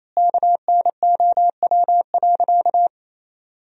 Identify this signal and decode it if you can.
Morse code, heard as a keyed tone, about 28 wpm, 700 Hz. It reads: KNOW.
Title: Sentences from Top 400 Words and Abbreviations 28wpm